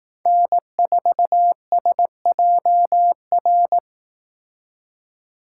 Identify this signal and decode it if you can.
Morse code, heard as a keyed tone, about 18 wpm, 700 Hz. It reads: N4SJR
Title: Speed Racing - USA Call Signs 15wpm